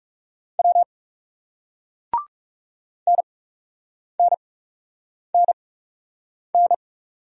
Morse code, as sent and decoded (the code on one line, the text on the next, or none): .-- / -. -. -. -.
W NNNN